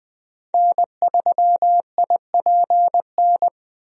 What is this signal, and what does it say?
Morse code, heard as a keyed tone, about 20 wpm, 700 Hz. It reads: N3IPN
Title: USA Call Signs - 5 Characters - Repeated 3x - 20wpm